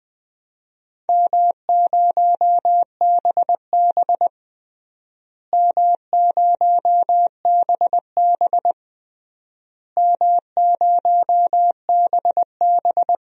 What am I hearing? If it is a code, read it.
Morse code, heard as a keyed tone, about 20 wpm, 700 Hz. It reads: M0BB M0BB M0BB
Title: DXCC-100 Call Signs - 4 Characters - Repeated 3x - 20wpm